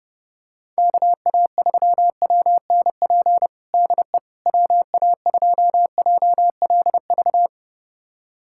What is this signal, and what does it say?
Morse code, heard as a keyed tone, about 32 wpm, 700 Hz. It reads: KA3WNP DE WA2JLV
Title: USA Callsign DE Callsign - 6 Characters - 30wpm